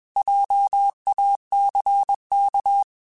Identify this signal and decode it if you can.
Morse code, heard as a keyed tone, about 21 wpm, 800 Hz. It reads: JACK